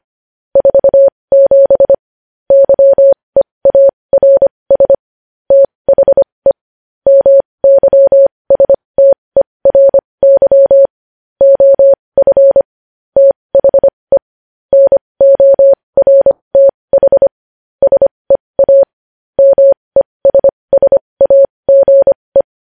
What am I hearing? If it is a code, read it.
Morse code, heard as a keyed tone, about 25 wpm, spaced as for 20 wpm, 550 Hz. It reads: 47 YEARS THE MYSTERY OF THE NORTH SEA MESSAGE
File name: podcast / News headlines-1x-25x20